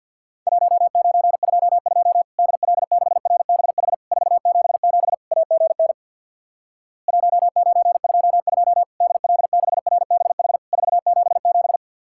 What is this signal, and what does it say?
Morse code, heard as a keyed tone, about 50 wpm, 700 Hz. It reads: 1922 DL6RBH 477 AGN 1922 DL6RBH 477
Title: Worked All Europe DX Contest - QTC Exchange with Repeats - 50wpm